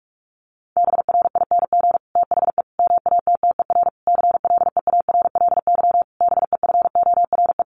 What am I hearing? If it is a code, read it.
Morse code, heard as a keyed tone, about 44 wpm, 700 Hz. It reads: BRING THE MATTER CLEARLY BEFORE
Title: Sentences from Top 1300 Words 45wpm